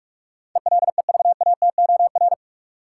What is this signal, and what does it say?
Morse code, heard as a keyed tone, about 42 wpm, 700 Hz. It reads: ELEVATOR
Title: Sentences from Top 1600 Words - New Word Drill - 45wpm